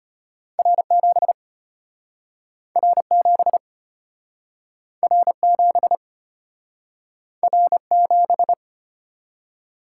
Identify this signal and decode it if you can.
Morse code, heard as a keyed tone, about 33 wpm, 700 Hz. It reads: R7 R7 R7 R7